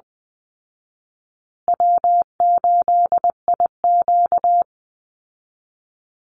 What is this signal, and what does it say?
Morse code, heard as a keyed tone, about 20 wpm, 700 Hz. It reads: W8IQ
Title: Easy Call Signs Repeated 3x - 20wpm